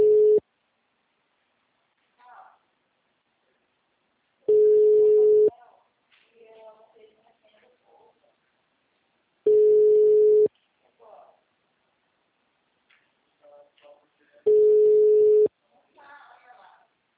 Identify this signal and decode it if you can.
Ringback tone (Europe, 425 Hz), with speech.